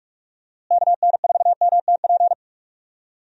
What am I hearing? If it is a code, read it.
Morse code, heard as a keyed tone, about 46 wpm, 700 Hz. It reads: KN4MTP
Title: USA Call Signs - 6 Characters - Repeated 3x - 45wpm